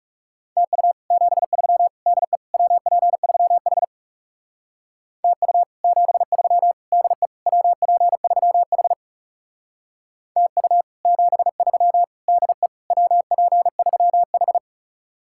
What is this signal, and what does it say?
Morse code, heard as a keyed tone, about 40 wpm, 700 Hz. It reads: TU 73 DE WP3H TU 73 DE WP3H TU 73 DE WP3H